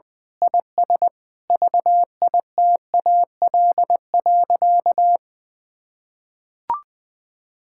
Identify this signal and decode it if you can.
Morse code, heard as a keyed tone, about 20 wpm, 700 Hz. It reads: IS VITAL.
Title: Sentences with 8-Word Phrases - Spoken Beginning - 20wpm